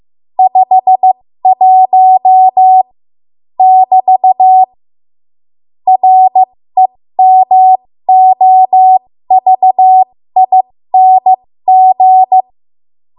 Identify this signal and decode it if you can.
Morse code, heard as a keyed tone, about 15 wpm, spaced as for 13 wpm, 750 Hz. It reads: 51 <BT> REMOVING
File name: Morse / 13WPM